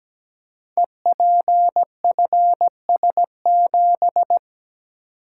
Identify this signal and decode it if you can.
Morse code, heard as a keyed tone, about 17 wpm, 700 Hz. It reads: EPFS7
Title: Code Groups - 5 Letter:Numbers - 17wpm